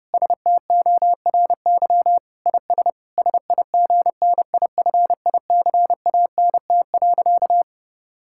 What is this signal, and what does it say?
Morse code, heard as a keyed tone, about 31 wpm, 700 Hz. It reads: STORY IS SIGNIFICANT.
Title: Sentences from Top 1000 Words - Review Part 6 - 30wpm